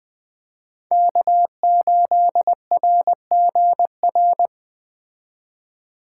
Morse code, heard as a keyed tone, about 20 wpm, 700 Hz. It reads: K8RGR
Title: Speed Racing - USA Call Signs 17wpm